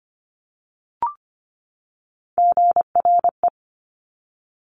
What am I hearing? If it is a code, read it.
Morse code, heard as a keyed tone, about 25 wpm, 700 Hz. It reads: GRE
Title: Top Three Letter Combinations in Words 25wpm